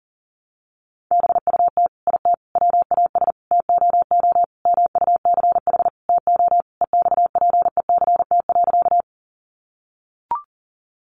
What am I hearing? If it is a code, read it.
Morse code, heard as a keyed tone, about 37 wpm, 700 Hz. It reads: BUT IT WAS TOO MUCH TO EXPECT.
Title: Sentences from Top 1100 Words 40wpm